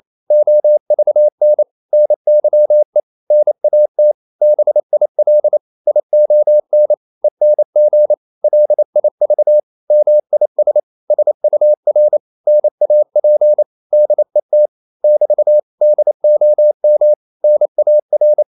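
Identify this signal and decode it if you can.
Morse code, heard as a keyed tone, about 28 wpm, 600 Hz. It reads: OVN NYE NAT BIL ION ENG LIV MIS SUR NAP DET <BT> DOM NAR
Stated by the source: Danish short words sent in Morse at 28 wpm, from CW med Gnister.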